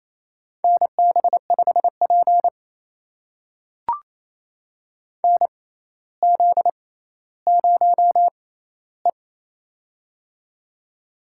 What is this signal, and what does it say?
Morse code, heard as a keyed tone, about 28 wpm, spaced as for 14 wpm, 700 Hz. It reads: NB5P NZ0E
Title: USA Call Signs - 4 Characters - 6x Character Spacing - 28wpm